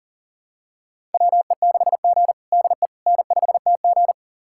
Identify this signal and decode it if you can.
Morse code, heard as a keyed tone, about 40 wpm, 700 Hz. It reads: WE6G DE N5TG
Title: USA Callsign DE Callsign - 4 Characters - 40wpm